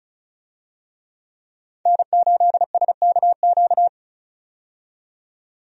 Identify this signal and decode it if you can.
Morse code, heard as a keyed tone, about 36 wpm, 700 Hz. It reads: N8SKQ
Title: USA Call Signs - 5 Characters - 35wpm